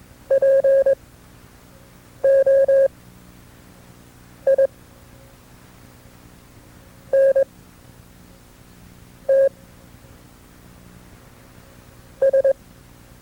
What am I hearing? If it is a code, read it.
Morse code, heard as a keyed tone, about 22 wpm, spaced as for 4 wpm, 540 Hz. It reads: POINTS
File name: FreeFiles / CW-practice / KB6MT-tape2-side2